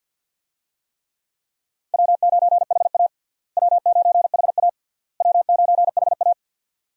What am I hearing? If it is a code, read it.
Morse code, heard as a keyed tone, about 51 wpm, 700 Hz. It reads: W9HA W9HA W9HA
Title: USA Call Signs - 4 Characters - Repeated 3x - 50wpm